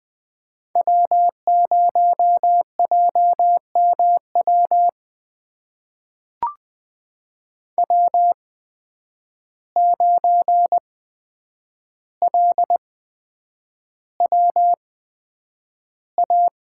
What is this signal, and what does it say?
Morse code, heard as a keyed tone, about 20 wpm, spaced as for 9 wpm, 700 Hz. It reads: W0JMW W9LWA
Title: USA Call Signs - 8x Character Spacing - 20wpm